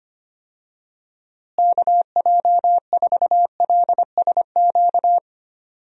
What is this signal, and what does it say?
Morse code, heard as a keyed tone, about 25 wpm, 700 Hz. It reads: KJ4LSQ
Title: USA Call Signs - 6 Characters - Repeated 2x - 25wpm